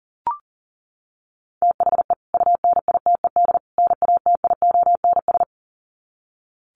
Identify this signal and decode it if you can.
Morse code, heard as a keyed tone, about 40 wpm, 700 Hz. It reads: THE UNITED NATIONS
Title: Sets of 3 Words 40wpm